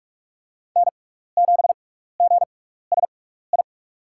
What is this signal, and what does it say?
Morse code, heard as a keyed tone, about 44 wpm, spaced as for 15 wpm, 700 Hz. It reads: N7GSI
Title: USA Call Signs - 5 Characters - 6x Character Spacing - 45wpm